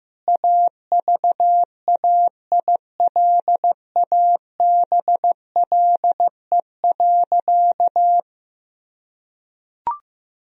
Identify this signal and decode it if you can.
Morse code, heard as a keyed tone, about 15 wpm, 700 Hz. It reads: AVAILABLE.
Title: Sentences from Top 1000 Words - Review Part 3 - 15wpm